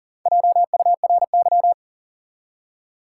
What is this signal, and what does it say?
Morse code, heard as a keyed tone, about 40 wpm, 700 Hz. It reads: JURY
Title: Four Letter Words - 40wpm